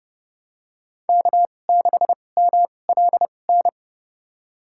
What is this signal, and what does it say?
Morse code, heard as a keyed tone, about 30 wpm, spaced as for 22 wpm, 700 Hz. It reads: K6MLN
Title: USA Call Signs - 5 Characters - 2x Character Spacing - 30wpm